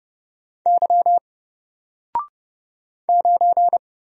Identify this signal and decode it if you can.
Morse code, heard as a keyed tone, about 31 wpm, 700 Hz. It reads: Y 9
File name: Single-Letter-Number-30wpm